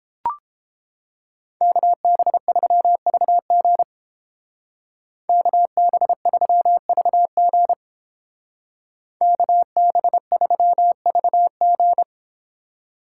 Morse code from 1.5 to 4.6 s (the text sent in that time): KB3VG